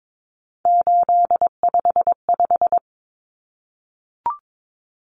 Morse code, heard as a keyed tone, about 22 wpm, 700 Hz. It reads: 855
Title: Numbers - 3-Digit 22wpm